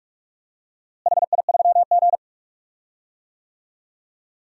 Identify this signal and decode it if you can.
Morse code, heard as a keyed tone, about 46 wpm, 700 Hz. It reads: HI3G
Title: DXCC-100 Call Signs - 4 Characters - Repeated 2x - 45wpm